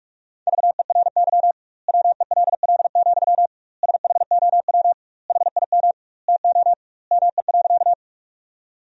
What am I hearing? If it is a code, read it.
Morse code, heard as a keyed tone, about 46 wpm, 700 Hz. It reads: VERY WELL, SHOW HIM TO ME.
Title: 2x Word-Spacing - Sentences from Top 200 Words - 45wpm